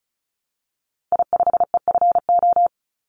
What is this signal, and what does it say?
Morse code, heard as a keyed tone, about 34 wpm, 700 Hz. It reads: I5EFO
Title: Call Signs Repeated 3x - 35wpm